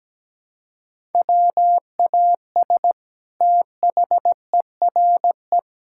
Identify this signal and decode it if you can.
Morse code, heard as a keyed tone, about 17 wpm, 700 Hz. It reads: WAS THERE
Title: Sentences from Top 1000 Words - Review Part 5 - 17wpm